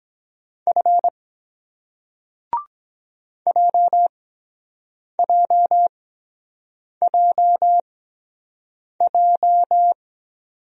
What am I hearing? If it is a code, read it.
Morse code, heard as a keyed tone, about 23 wpm, spaced as for 7 wpm, 700 Hz. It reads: F JJJJ